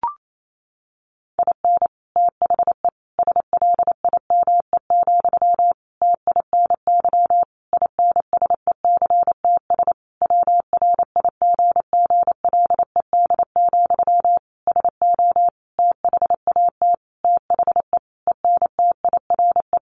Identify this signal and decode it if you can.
Morse code, heard as a keyed tone, about 28 wpm, 700 Hz. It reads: IN THE SLIME, TINY INSECTS WRIGGLED, SO THAT THE ENTIRE
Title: A Sound Of Thunder 28wpm